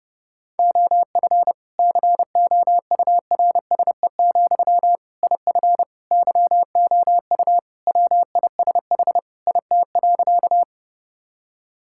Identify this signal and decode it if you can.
Morse code, heard as a keyed tone, about 30 wpm, 700 Hz. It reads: OF COURSE, IF YOU WISH IT.
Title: Sentences from Top 1600 Words 30wpm